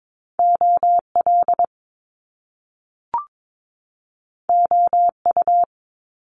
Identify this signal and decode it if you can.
Morse code, heard as a keyed tone, about 22 wpm, 700 Hz. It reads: OL OU
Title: Top Two Letter Combinations in Words 22wpm